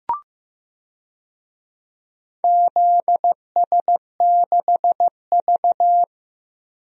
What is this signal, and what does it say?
Morse code, heard as a keyed tone, about 15 wpm, 700 Hz. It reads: ZS6V
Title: DXCC-100 Call Signs - 4 Characters - Repeated 2x - 15wpm